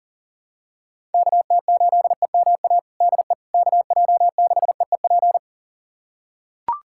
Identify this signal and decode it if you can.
Morse code, heard as a keyed tone, about 40 wpm, 700 Hz. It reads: KT8EMA DE KJ6EEP
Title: USA Callsign DE Callsign 40wpm